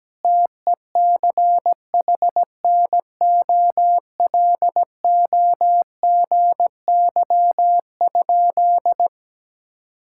Morse code, heard as a keyed tone, about 17 wpm, 700 Hz. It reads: TECHNOLOGY?
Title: Sentences from Top 1000 Words - Review Part 5 - 17wpm